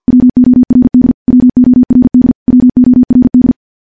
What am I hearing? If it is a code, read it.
Morse code, heard as a keyed tone, about 50 wpm, 250 Hz. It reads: WORD WORD WORD
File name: morse-training-50WPM